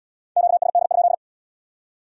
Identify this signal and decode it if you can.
Morse code, heard as a keyed tone, about 75 wpm, 700 Hz. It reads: QSD?